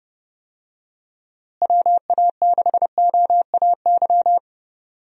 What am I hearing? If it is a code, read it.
Morse code, heard as a keyed tone, about 30 wpm, 700 Hz. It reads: WA6OAY
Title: USA Call Signs - 6 Characters - Repeated 2x - 30wpm